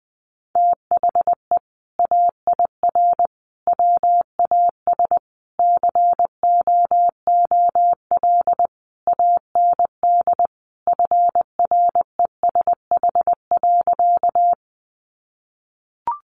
Morse code, heard as a keyed tone, about 20 wpm, 700 Hz. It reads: THE AIR WAS COOL AND FRESH.
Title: Sentences from Top 1200 Words 20wpm